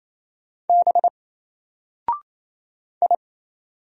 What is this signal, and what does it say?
Morse code, heard as a keyed tone, about 28 wpm, 700 Hz. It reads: B I
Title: Single Character (Letter:Number:Punctuation:Prosign) Alt 28wpm